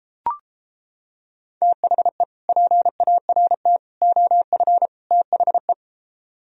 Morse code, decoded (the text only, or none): THE PART OF THE